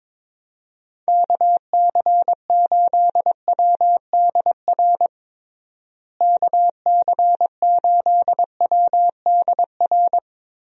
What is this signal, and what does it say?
Morse code, heard as a keyed tone, about 22 wpm, 700 Hz. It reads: KC8WDR KC8WDR
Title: USA Call Signs - 6 Characters - Repeated 2x - 22wpm